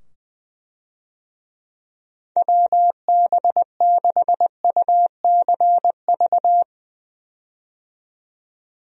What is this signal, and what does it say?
Morse code, heard as a keyed tone, about 20 wpm, 700 Hz. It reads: WB6UCV
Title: USA Call Signs - 6 Characters - 20wpm